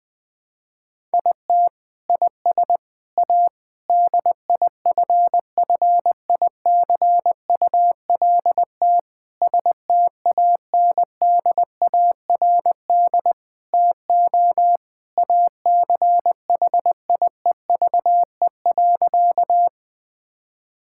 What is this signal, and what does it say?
Morse code, heard as a keyed tone, about 20 wpm, 700 Hz. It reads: IT IS A DIFFICULT STANDARD TO ACHIEVE.
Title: Sentences from Top 1000 Words - Review Part 6 - 20wpm